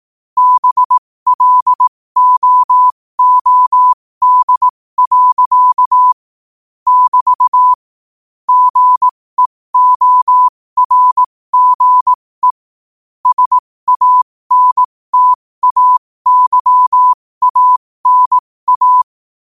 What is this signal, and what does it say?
Morse code, heard as a keyed tone, about 18 wpm, spaced as for 15 wpm, 1000 Hz. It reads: BLOOD. <BT> GEORGE SANTAYANA
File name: qotdfiles / 15wpm